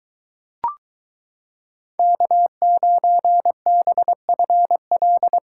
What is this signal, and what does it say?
Morse code, heard as a keyed tone, about 22 wpm, 700 Hz. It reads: K9BFL